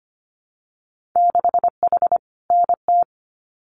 Morse code, heard as a keyed tone, about 25 wpm, 700 Hz. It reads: 6H NT
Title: Winter Field Day Exchanges Encore 25wpm